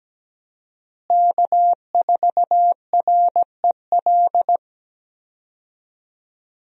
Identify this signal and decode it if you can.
Morse code, heard as a keyed tone, about 17 wpm, 700 Hz. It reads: K4REL
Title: USA Call Signs - 5 Characters - Repeated 3x - 17wpm